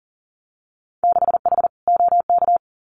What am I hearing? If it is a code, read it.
Morse code, heard as a keyed tone, about 38 wpm, 700 Hz. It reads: 6H OK
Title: Winter Field Day Exchanges Encore 40wpm